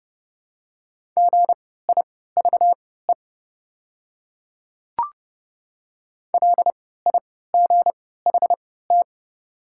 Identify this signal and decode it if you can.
Morse code, heard as a keyed tone, about 30 wpm, spaced as for 17 wpm, 690 Hz. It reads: GIVE LIGHT
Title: 3x Character Spacing - Top 300 Words - 30wpm